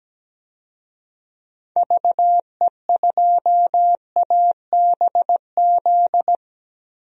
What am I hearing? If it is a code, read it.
Morse code, heard as a keyed tone, about 17 wpm, 700 Hz. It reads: VE2ABZ
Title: DXCC-100 Call Signs - 6 Characters - Repeated 2x - 17wpm